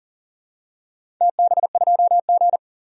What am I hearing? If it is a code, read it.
Morse code, heard as a keyed tone, about 40 wpm, 700 Hz. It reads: TB2G
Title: DXCC-100 Call Signs - 4 Characters - 40wpm